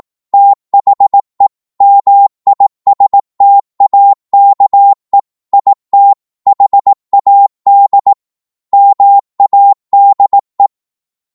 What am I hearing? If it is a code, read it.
Morse code, heard as a keyed tone, about 18 wpm, 800 Hz. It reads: THE MISTAKE IT HAD MADE